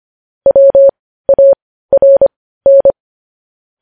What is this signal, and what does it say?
Morse code, heard as a keyed tone, about 25 wpm, spaced as for 15 wpm, 550 Hz. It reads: WARN
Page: News headlines in Morse code at 15 wpm